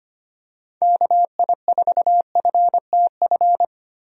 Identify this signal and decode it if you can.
Morse code, heard as a keyed tone, about 25 wpm, 700 Hz. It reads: KI4FTF
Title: USA Call Signs - 6 Characters - 25wpm